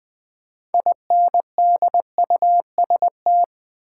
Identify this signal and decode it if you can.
Morse code, heard as a keyed tone, about 20 wpm, 700 Hz. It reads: INDUST
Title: Build A Word - 3 Letters At A Time - 8-Letter Words - 20wpm